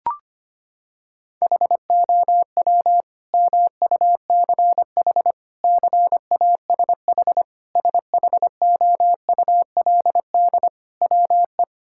Morse code, heard as a keyed tone, about 25 wpm, 700 Hz. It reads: HOW MUCH CASH SHOULD WE
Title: Sentences from Top 1000 Words - Review Part 5 - 25wpm